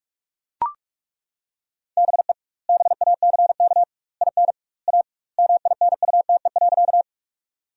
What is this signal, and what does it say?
Morse code, heard as a keyed tone, about 41 wpm, 700 Hz. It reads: BE BACK IN A MINUTE.
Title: 2x Word-Spacing - Sentences from Top 500 Words - 45wpm